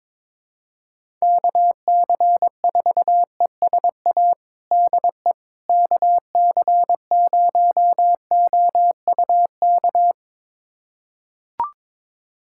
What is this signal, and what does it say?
Morse code, heard as a keyed tone, about 22 wpm, 700 Hz. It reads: KC4ESA DE KC0OUK
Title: USA Callsign DE Callsign - 6 Characters - 22wpm